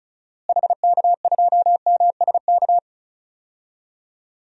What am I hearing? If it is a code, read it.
Morse code, heard as a keyed tone, about 36 wpm, 700 Hz. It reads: HK2MSK
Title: DXCC-100 Call Signs - 6 Characters - Repeated 3x - 35wpm